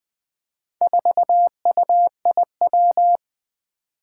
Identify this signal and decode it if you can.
Morse code, heard as a keyed tone, about 20 wpm, 700 Hz. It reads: 4UIW
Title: Four Challenging Letter-Number Combinations 20wpm